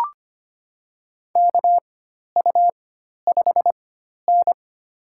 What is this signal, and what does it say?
Morse code, heard as a keyed tone, about 25 wpm, spaced as for 12 wpm, 700 Hz. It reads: KU5N
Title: USA Call Signs - 4x Character Spacing - 25wpm